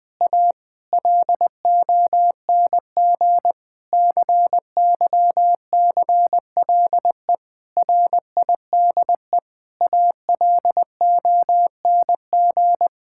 Morse code, heard as a keyed tone, about 20 wpm, 700 Hz. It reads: A LONG CYCLE RIDE ALONG
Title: Sentences from Top 1000 Words - Review Part 6 - 20wpm